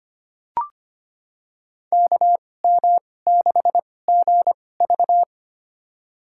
Morse code, heard as a keyed tone, about 25 wpm, spaced as for 18 wpm, 700 Hz. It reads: KM6GV
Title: USA Call Signs - 2x Character Spacing - 25wpm